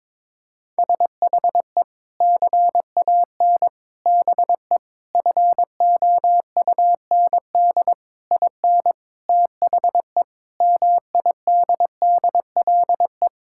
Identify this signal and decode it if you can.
Morse code, heard as a keyed tone, about 21 wpm, 700 Hz. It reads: SHE CAN BE FOUND IN THE MIDDLE
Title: Sentences from Top 1000 Words - Review Part 6 - 22wpm